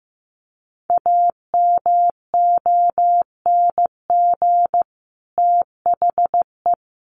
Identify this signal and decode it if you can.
Morse code, heard as a keyed tone, about 15 wpm, 700 Hz. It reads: AMONG THE
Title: Sets of 2 Words 15wpm